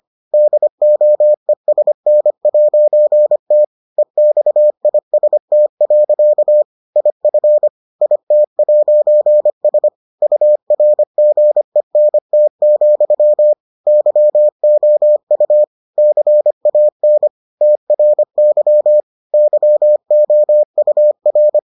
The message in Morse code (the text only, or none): DOESN'T EXIST. IF IT'S URGENT, YOU CAN TRY YOUR